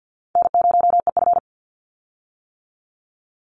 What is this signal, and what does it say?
Morse code, heard as a keyed tone, about 50 wpm, 700 Hz. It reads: N0EF
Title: Call Signs - 50wpm